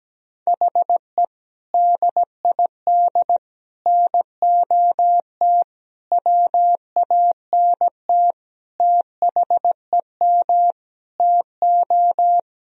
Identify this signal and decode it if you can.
Morse code, heard as a keyed tone, about 17 wpm, 700 Hz. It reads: HE DID NOT WANT THEM TO
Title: Sentences from Top 1000 Words - Review Part 1 - 17wpm